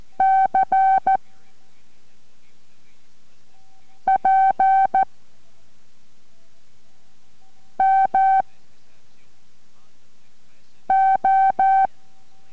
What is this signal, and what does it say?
Morse code, heard as a keyed tone, about 14 wpm, 770 Hz. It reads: C P M O